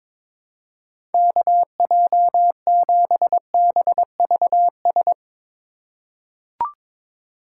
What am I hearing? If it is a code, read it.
Morse code, heard as a keyed tone, about 22 wpm, 700 Hz. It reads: KJ7BVS
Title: USA Call Signs - 6 Characters - 3x Character Spacing - 22wpm